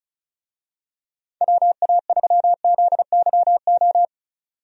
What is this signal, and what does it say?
Morse code, heard as a keyed tone, about 34 wpm, 700 Hz. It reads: WA3ZYO
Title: USA Call Signs - 6 Characters - 35wpm